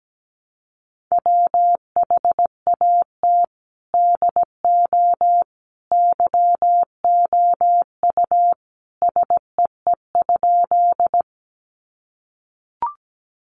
Morse code, heard as a keyed tone, about 17 wpm, 700 Hz. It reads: WHAT DO YOU SEE?
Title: Sentences from Top 100 Words 17wpm